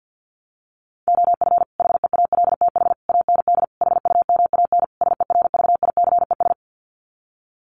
Morse code, heard as a keyed tone, about 50 wpm, 700 Hz. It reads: OF HEALTH AND HUMAN SERVICES
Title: Sets of 5 Words 50wpm